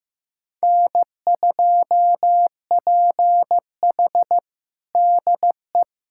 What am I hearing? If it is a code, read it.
Morse code, heard as a keyed tone, about 15 wpm, 700 Hz. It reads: N2PH DE
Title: USA Callsign DE Callsign - 4 Characters - 15wpm